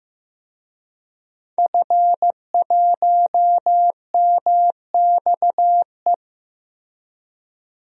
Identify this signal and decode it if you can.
Morse code, heard as a keyed tone, about 15 wpm, 700 Hz. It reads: F1MXE
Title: DXCC-100 Call Signs - 5 Characters - Repeated 3x - 15wpm